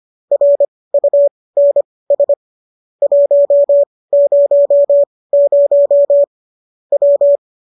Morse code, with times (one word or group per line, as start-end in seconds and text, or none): 0.3-2.3 RUNS
3.0-6.2 100
6.9-7.4 W